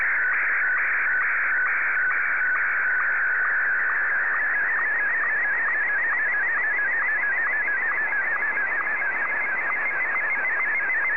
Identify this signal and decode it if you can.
SSTV (slow-scan television), Martin 1.